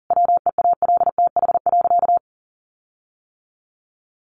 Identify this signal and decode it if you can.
Morse code, heard as a keyed tone, about 40 wpm, 700 Hz. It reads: WEALTH.
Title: Sentences from Top 1500 Words 40wpm